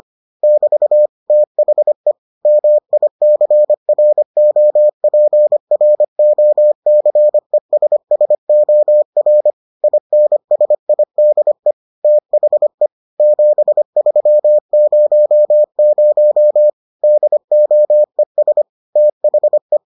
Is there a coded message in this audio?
Yes: Morse code from 0.4 to 19.8 s, heard as a keyed tone, about 25 wpm, 600 Hz. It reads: <BT> THE MICROPROCESSOR INSIDE THE 7300 DOES THE